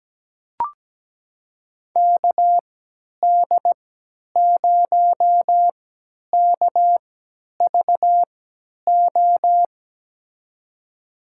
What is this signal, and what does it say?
Morse code, heard as a keyed tone, about 17 wpm, spaced as for 10 wpm, 700 Hz. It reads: KD0KVO
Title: USA Call Signs - 3x Character Spacing - 17wpm